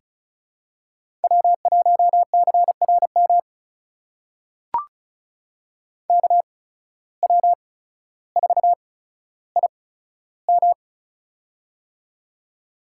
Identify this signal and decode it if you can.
Morse code, heard as a keyed tone, about 36 wpm, spaced as for 15 wpm, 690 Hz. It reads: W1CRM KW4IM
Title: USA Call Signs - 5 Characters - 8x Character Spacing - 35wpm